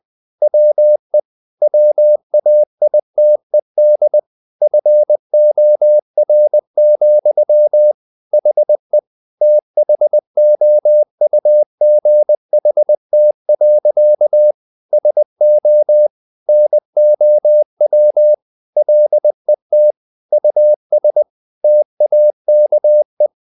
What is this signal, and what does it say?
Morse code, heard as a keyed tone, about 20 wpm, 600 Hz. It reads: WE WAITED FOR, HE THOUGHT. SO NOW LET US TAKE
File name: the old man and the sea part 2 20wpm